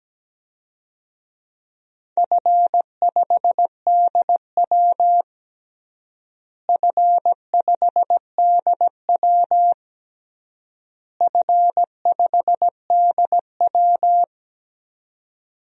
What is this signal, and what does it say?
Morse code, heard as a keyed tone, about 17 wpm, 700 Hz. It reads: F5DW F5DW F5DW
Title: DXCC-100 Call Signs - 4 Characters - Repeated 3x - 17wpm